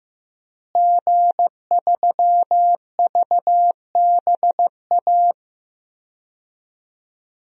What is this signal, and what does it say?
Morse code, heard as a keyed tone, about 15 wpm, 700 Hz. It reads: G3VBA
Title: DXCC-100 Call Signs - 5 Characters - Repeated 3x - 15wpm